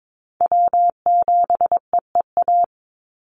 Morse code, heard as a keyed tone, about 22 wpm, 700 Hz. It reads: W7EEA
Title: USA Call Signs 22wpm